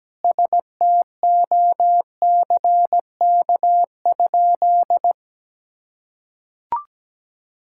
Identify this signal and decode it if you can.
Morse code, heard as a keyed tone, about 17 wpm, 700 Hz. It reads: STOCK?
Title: Sentences from Top 1000 Words - Review Part 5 - 17wpm